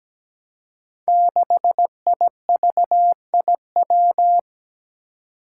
Morse code, heard as a keyed tone, about 17 wpm, 700 Hz. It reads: 6IVIW
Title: Five Challenging Letter-Number Combinations 17wpm